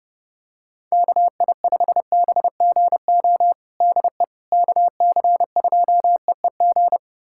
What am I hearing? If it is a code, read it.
Morse code, heard as a keyed tone, about 32 wpm, 700 Hz. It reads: KI5BGO DE KC2EEG
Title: USA Callsign DE Callsign 30wpm